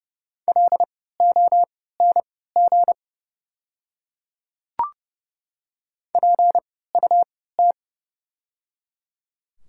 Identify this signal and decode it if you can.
Morse code, heard as a keyed tone, about 30 wpm, spaced as for 17 wpm, 700 Hz. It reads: LONG PUT
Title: 3x Character Spacing - Top 300 Words - 30wpm